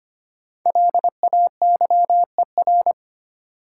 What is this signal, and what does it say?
Morse code, heard as a keyed tone, about 25 wpm, 700 Hz. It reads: LAYER
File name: 5 Letter Words - First 3 Letters Given - 25wpm